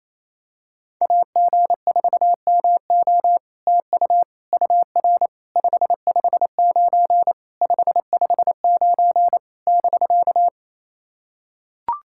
Morse code, heard as a keyed tone, about 28 wpm, 700 Hz. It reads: AG4MO TU UR 559 559 <BK>
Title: SOTA QSO Practice 28wpm